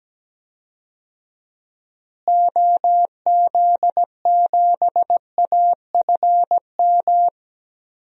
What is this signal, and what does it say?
Morse code, heard as a keyed tone, about 17 wpm, 700 Hz. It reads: OZ7AFM
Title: DXCC-100 Call Signs - 6 Characters - Repeated 3x - 17wpm